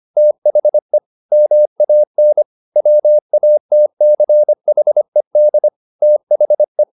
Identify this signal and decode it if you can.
Morse code, heard as a keyed tone, about 25 wpm, 600 Hz. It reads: THE MAN WATCHED THE
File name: the old man and the sea part 1 25wpm